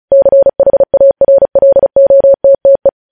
Morse code, heard as a keyed tone, about 36 wpm, 550 Hz. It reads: CHARLOTTE